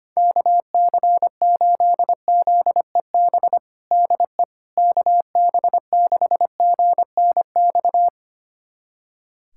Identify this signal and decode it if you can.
Morse code, heard as a keyed tone, about 25 wpm, 700 Hz. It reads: KC8ZEB DE KB6GNX
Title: USA Callsign DE Callsign 25wpm